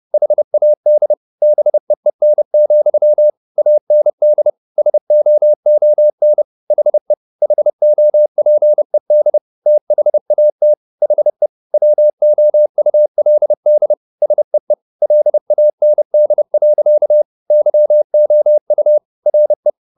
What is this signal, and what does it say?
Morse code, heard as a keyed tone, about 30 wpm, 600 Hz. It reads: HAD BEEN, AND SOON HE HOPED THAT HE WOULD SEE LAND. YOU RE